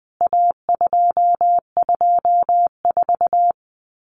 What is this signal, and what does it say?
Morse code, heard as a keyed tone, about 20 wpm, 700 Hz. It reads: A224
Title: CWT Exchange 20wpm